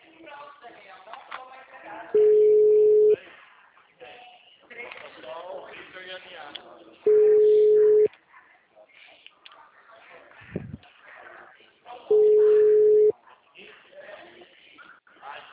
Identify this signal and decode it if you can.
Ringback tone (Europe, 425 Hz), with speech.
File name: out- -2011- - - .9852